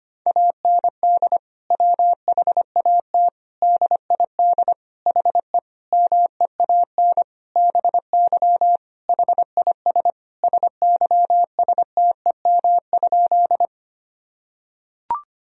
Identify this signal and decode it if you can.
Morse code, heard as a keyed tone, about 25 wpm, 700 Hz. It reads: AND WHAT DID HE MEAN BY HIS SYSTEM?
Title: Sentences from Top 1000 Words - Review Part 3 - 25wpm